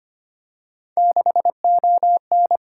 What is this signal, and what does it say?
Morse code, heard as a keyed tone, about 25 wpm, 700 Hz. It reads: 6ON
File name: Three Letter-Number Combinations - 25wpm